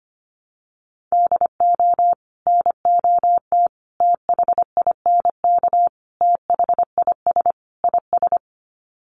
Morse code, heard as a keyed tone, about 25 wpm, 700 Hz. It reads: DO NOT THINK THIS IS
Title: Sets of 5 Words Encore2 25wpm